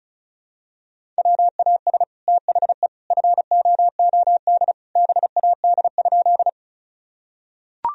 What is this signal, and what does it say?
Morse code, heard as a keyed tone, about 34 wpm, 700 Hz. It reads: WAS THE FOOD BAD?
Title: Sentences from Top 1000 Words - Review Part 1 - 35wpm